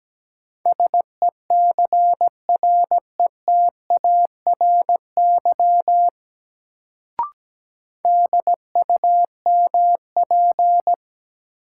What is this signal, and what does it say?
Morse code, heard as a keyed tone, about 17 wpm, 700 Hz. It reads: SECRETARY DUMP
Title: Sentences from Top 1300 Words - New Word Drill - 17wpm